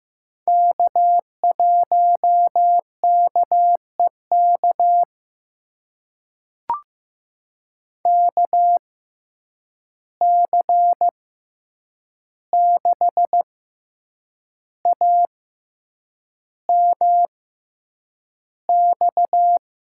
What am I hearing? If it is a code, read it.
Morse code, heard as a keyed tone, about 15 wpm, spaced as for 7 wpm, 700 Hz. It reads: K1KEK KC6AMX
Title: USA Call Signs - 6x Character Spacing - 15wpm